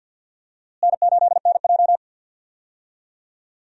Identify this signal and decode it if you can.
Morse code, heard as a keyed tone, about 52 wpm, 700 Hz. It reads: N8NJ
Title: USA Call Signs - 4 Characters - Repeated 2x - 50wpm